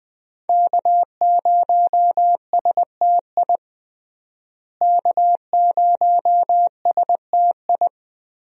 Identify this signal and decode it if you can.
Morse code, heard as a keyed tone, about 20 wpm, 700 Hz. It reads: K0STI K0STI
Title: USA Call Signs - 5 Characters - Repeated 2x - 20wpm